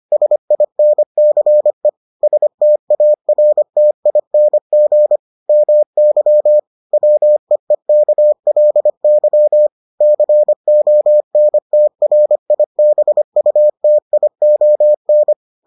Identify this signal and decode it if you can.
Morse code, heard as a keyed tone, about 25 wpm, 600 Hz. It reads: SINCE STARTING MY WEEKLY CONTRIBUTION